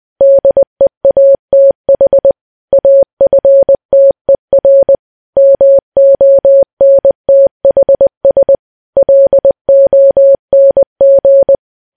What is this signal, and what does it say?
Morse code, heard as a keyed tone, about 20 wpm, 550 Hz. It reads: DEATH AFTER MONTHS LONG
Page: news-headlines-in-morse-code-at-20-wpm